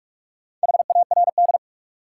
Morse code, heard as a keyed tone, about 46 wpm, 700 Hz. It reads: HARD
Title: Sentences from Top 300 Words - New Word Drill - 45wpm